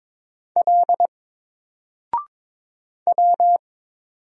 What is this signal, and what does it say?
Morse code, heard as a keyed tone, about 22 wpm, 700 Hz. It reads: L W
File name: Single-Letter-Number-22wpm